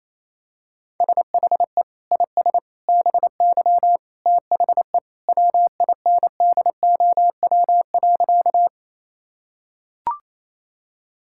Morse code, heard as a keyed tone, about 28 wpm, 700 Hz. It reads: SHE IS BY THE WINDOW.
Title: Sentences from Top 1600 Words 28wpm